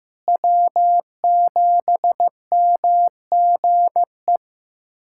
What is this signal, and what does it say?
Morse code, heard as a keyed tone, about 15 wpm, 700 Hz. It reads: W7MGE
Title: USA Callsign DE Callsign - 5 Characters - 15wpm